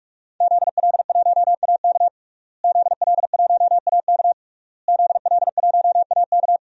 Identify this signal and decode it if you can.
Morse code, heard as a keyed tone, about 46 wpm, 700 Hz. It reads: ZL1AK ZL1AK ZL1AK
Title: DXCC-100 Call Signs - 5 Characters - Repeated 3x - 45wpm